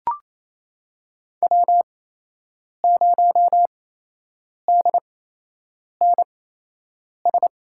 Morse code, heard as a keyed tone, about 28 wpm, spaced as for 8 wpm, 700 Hz. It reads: W0DNS
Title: USA Call Signs - 8x Character Spacing - 28wpm